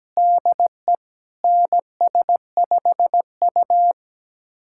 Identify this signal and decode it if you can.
Morse code, heard as a keyed tone, about 17 wpm, 700 Hz. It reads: DE NS5U
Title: USA Callsign DE Callsign - 4 Characters - 17wpm